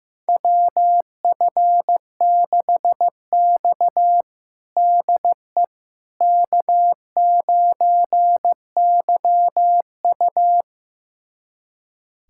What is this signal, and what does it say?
Morse code, heard as a keyed tone, about 16 wpm, 700 Hz. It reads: WF6X DE K9YU
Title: USA Callsign DE Callsign - 4 Characters - 15wpm